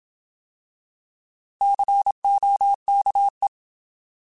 Morse code, heard as a keyed tone, about 27 wpm, 800 Hz. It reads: COKE
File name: Alphabetical / Alphabetical mp3s / Cs @25